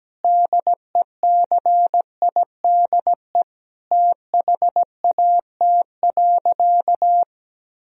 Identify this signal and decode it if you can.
Morse code, heard as a keyed tone, about 17 wpm, 700 Hz. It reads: DECIDE THAT.
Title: Sentences from Top 1000 Words - Review Part 3 - 17wpm